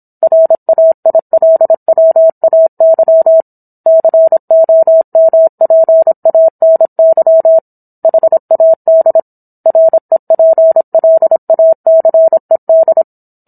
Morse code, heard as a keyed tone, about 27 wpm, 650 Hz. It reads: RAILWAY COMPANY HAD REPLACED